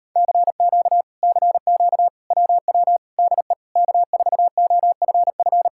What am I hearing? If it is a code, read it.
Morse code, heard as a keyed tone, about 38 wpm, 700 Hz. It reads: CQ CQ WW DE K4OFF